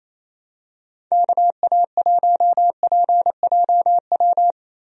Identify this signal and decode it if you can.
Morse code, heard as a keyed tone, about 28 wpm, 700 Hz. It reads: KA1PJW
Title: USA Call Signs - 6 Characters - Repeated 3x - 28wpm